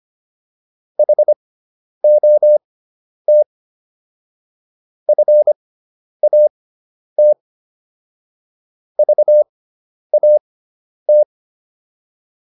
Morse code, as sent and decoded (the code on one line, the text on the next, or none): .... --- - / ..-. .- - / ...- .- -
HOT FAT VAT